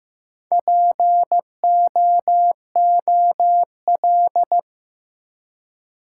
Morse code, heard as a keyed tone, about 16 wpm, 700 Hz. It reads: POOL
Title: Four Letter Words - 15wpm